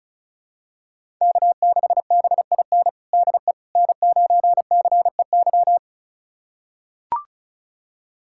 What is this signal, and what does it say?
Morse code, heard as a keyed tone, about 34 wpm, 700 Hz. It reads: K6BIN DE N9CEY
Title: USA Callsign DE Callsign - 5 Characters - 35wpm